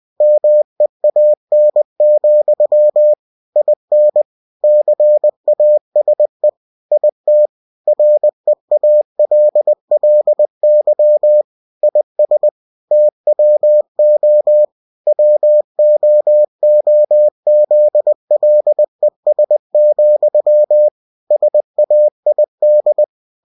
Morse code, heard as a keyed tone, about 20 wpm, 600 Hz. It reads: MEAN, IN CASE IT REALLY IS TWO WOOZLES, SAID
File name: cw-books / winnie-the-pooh / winnie-20wpm-page-0022